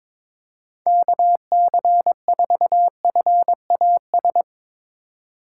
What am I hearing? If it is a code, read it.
Morse code, heard as a keyed tone, about 22 wpm, 700 Hz. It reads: KC4FAS
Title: USA Call Signs - 6 Characters - 2x Character Spacing - 22wpm